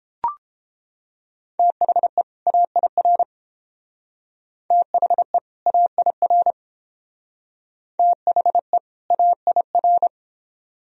Morse code, heard as a keyed tone, about 29 wpm, 700 Hz. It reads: THE AIR THE AIR THE AIR